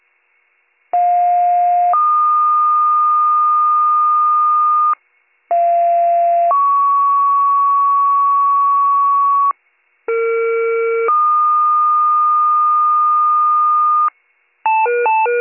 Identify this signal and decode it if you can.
Two-tone page, heard as a one-second tone, then a three-second tone, about 710 then 1220 Hz, about 690 then 1120 Hz, about 460 then 1220 Hz. It is heard from 0.9 to 14.1 s.